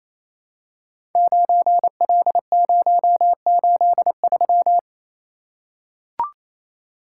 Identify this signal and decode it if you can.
Morse code, heard as a keyed tone, about 27 wpm, 700 Hz. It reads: 9L083
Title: Code Groups - 5 Letter:Numbers - 28wpm